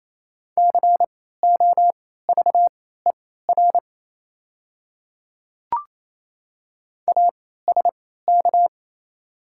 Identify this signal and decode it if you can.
Morse code, heard as a keyed tone, about 28 wpm, spaced as for 16 wpm, 700 Hz. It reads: COVER ASK
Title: 3x Character Spacing - Top 300 Words - 28wpm